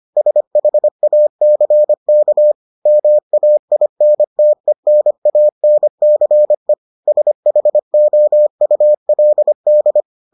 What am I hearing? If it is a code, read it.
Morse code, heard as a keyed tone, about 25 wpm, 600 Hz. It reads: SHACK MAINTENANCE SHOULD